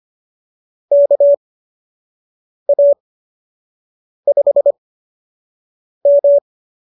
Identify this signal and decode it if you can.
Morse code, heard as a keyed tone, about 25 wpm, spaced as for 6 wpm, 590 Hz. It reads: KA5M